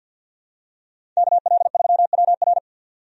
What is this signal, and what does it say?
Morse code, heard as a keyed tone, about 53 wpm, 700 Hz. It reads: KL3WR
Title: USA Call Signs - 5 Characters - 50wpm